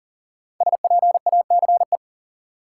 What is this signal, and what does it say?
Morse code, heard as a keyed tone, about 40 wpm, 700 Hz. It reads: SPACE
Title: Sentences from Top 1500 Words - New Word Drill Alt - 40wpm